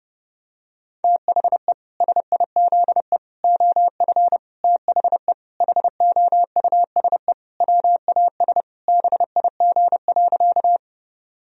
Morse code, heard as a keyed tone, about 32 wpm, 700 Hz. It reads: THE SIZE OF THE HOUSE WAS BIG.
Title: Sentences from Top 1000 Words - Review Part 2 - 30wpm